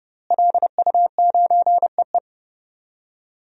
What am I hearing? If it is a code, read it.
Morse code, heard as a keyed tone, about 31 wpm, 700 Hz. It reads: LU9EE
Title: DXCC-100 Call Signs - 5 Characters - Repeated 2x - 30wpm